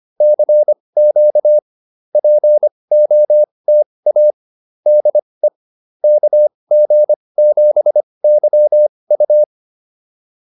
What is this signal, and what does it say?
Morse code, heard as a keyed tone, about 25 wpm, spaced as for 20 wpm, 600 Hz. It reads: CQ POTA DE KG7YU